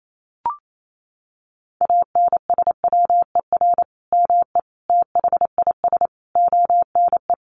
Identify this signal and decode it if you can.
Morse code, heard as a keyed tone, about 28 wpm, 700 Hz. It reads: ANSWER ME THIS ONE
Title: Sentences from Top 400 Words 28wpm